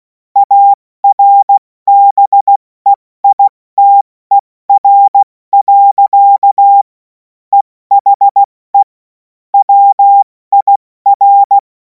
Morse code, heard as a keyed tone, about 16 wpm, spaced as for 14 wpm, 800 Hz. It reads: ARBEITER. EHE WIR
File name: Morsekurs / Witze / Witz 10. März 2026 70 F80